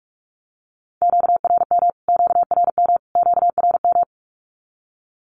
Morse code, heard as a keyed tone, about 44 wpm, 700 Hz. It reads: QRM QRM QRM
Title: Q-Codes Repeated 3x 45wpm